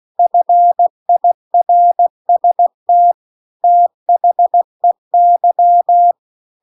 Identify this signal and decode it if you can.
Morse code, heard as a keyed tone, about 16 wpm, 700 Hz. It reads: FIRST THEY
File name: CW-The-Big-Twoheaded-River-2-16WPM